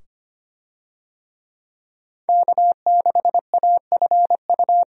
Morse code, heard as a keyed tone, about 25 wpm, 700 Hz. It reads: K6AFU
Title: USA Call Signs - 5 Characters - 25wpm